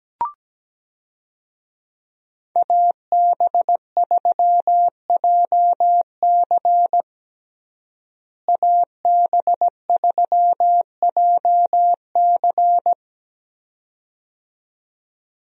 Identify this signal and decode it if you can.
Morse code, heard as a keyed tone, about 17 wpm, 700 Hz. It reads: AB3JC AB3JC
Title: USA Call Signs - 5 Characters - Repeated 2x - 17wpm